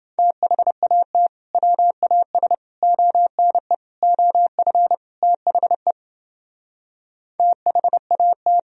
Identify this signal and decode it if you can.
Morse code, heard as a keyed tone, about 31 wpm, 700 Hz. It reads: THAT WAS ONE OF THE THAT